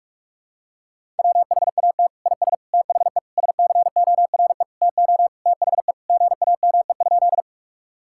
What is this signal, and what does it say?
Morse code, heard as a keyed tone, about 46 wpm, 700 Hz. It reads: WHAT IS THE SCORE TO THE GAME?
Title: Sentences from Top 1600 Words 45wpm